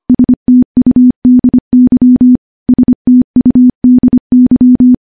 Morse code, heard as a keyed tone, about 25 wpm, 250 Hz. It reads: STUDY STUDY
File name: morse / morse-training-25WPM